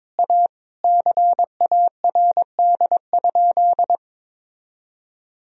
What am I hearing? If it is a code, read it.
Morse code, heard as a keyed tone, about 22 wpm, 700 Hz. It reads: A CARD?
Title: Sentences from Top 1000 Words - Review Part 4 - 22wpm